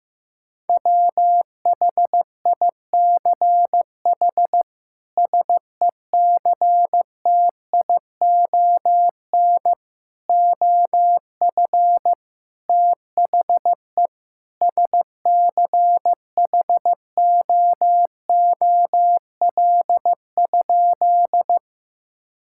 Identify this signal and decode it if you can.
Morse code, heard as a keyed tone, about 15 wpm, 700 Hz. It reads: WHICH SECTION OF THE SCHOOL?
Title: Sentences from Top 1000 Words - Review Part 5 - 15wpm